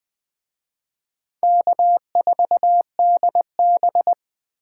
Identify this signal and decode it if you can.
Morse code, heard as a keyed tone, about 23 wpm, 700 Hz. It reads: K4DB
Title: Speed Racing - USA 1x2 Call Signs 20wpm